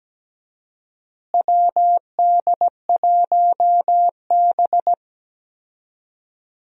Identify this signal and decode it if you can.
Morse code, heard as a keyed tone, about 17 wpm, 700 Hz. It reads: WD1B
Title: USA Call Signs - 4 Characters - Repeated 2x - 17wpm